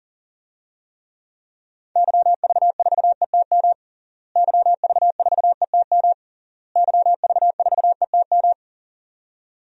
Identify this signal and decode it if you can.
Morse code, heard as a keyed tone, about 40 wpm, 700 Hz. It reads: YV4ETM YV4ETM YV4ETM
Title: DXCC-100 Call Signs - 6 Characters - Repeated 3x - 40wpm